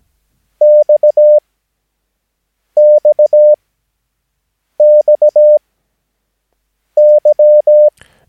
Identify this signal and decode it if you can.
Morse code, heard as a keyed tone, about 17 wpm, spaced as for 6 wpm, 600 Hz. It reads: XXXY
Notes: Övningsmaterial med alfabetet A till Ö samt siffror 0 till 9 enligt International/Continental Morse Code.